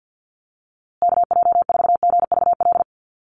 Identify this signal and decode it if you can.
Morse code, heard as a keyed tone, about 50 wpm, 700 Hz. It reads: KJ4GVL
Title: USA Call Signs Repeated 3x 50wpm